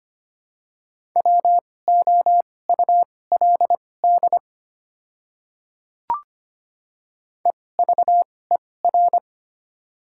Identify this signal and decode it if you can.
Morse code, heard as a keyed tone, about 25 wpm, spaced as for 18 wpm, 700 Hz. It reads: WOULD EVER
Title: 2x Character Spacing - Top 500 Words - 25wpm